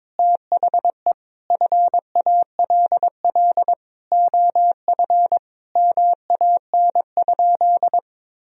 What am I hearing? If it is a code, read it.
Morse code, heard as a keyed tone, about 22 wpm, 700 Hz. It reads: THE FALL OF MAN?
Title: Sentences from Top 1000 Words - Review Part 2 - 22wpm